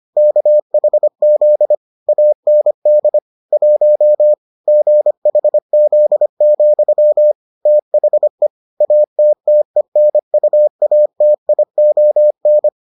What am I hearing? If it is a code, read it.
Morse code, heard as a keyed tone, about 25 wpm, 600 Hz. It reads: KHZ AND 1 GHZ, THE ATTENUATION